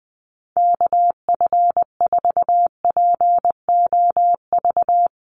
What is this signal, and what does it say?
Morse code, heard as a keyed tone, about 20 wpm, 700 Hz. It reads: KF4POV
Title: USA Call Signs Repeated 3x 20wpm